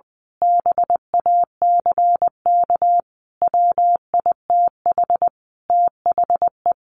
Morse code, heard as a keyed tone, about 20 wpm, 700 Hz. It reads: BACK WITH THE
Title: A Sound Of Thunder 20wpm